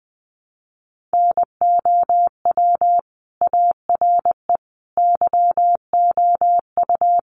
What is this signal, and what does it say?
Morse code, heard as a keyed tone, about 20 wpm, 700 Hz. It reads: NOW ARE YOU
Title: Sentences from Top 900 Words 20wpm